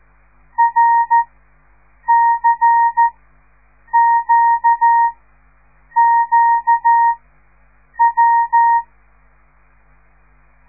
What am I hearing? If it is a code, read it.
Morse code, heard as a keyed tone, about 13 wpm, spaced as for 7 wpm, 930 Hz. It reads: RCQQW
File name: radiomorningstar / morse / les 5